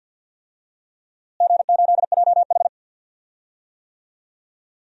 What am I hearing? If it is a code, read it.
Morse code, heard as a keyed tone, about 52 wpm, 700 Hz. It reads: G8JH
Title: DXCC-100 Call Signs - 4 Characters - Repeated 3x - 50wpm